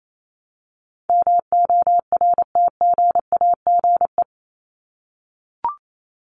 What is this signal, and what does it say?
Morse code, heard as a keyed tone, about 28 wpm, 700 Hz. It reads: MORTGAGE
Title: Top 1751-2000 Words 28wpm